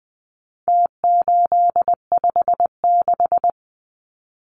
Morse code, heard as a keyed tone, about 20 wpm, 700 Hz. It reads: T856
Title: Numbers - Contest Style - 4-Digit 20wpm